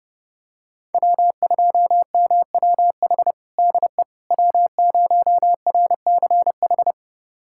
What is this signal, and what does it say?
Morse code, heard as a keyed tone, about 30 wpm, 700 Hz. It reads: W2MWH DE W0RCH
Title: USA Callsign DE Callsign - 5 Characters - 30wpm